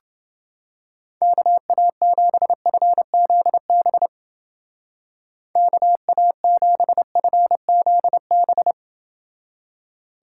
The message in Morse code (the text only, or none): KA7FZB KA7FZB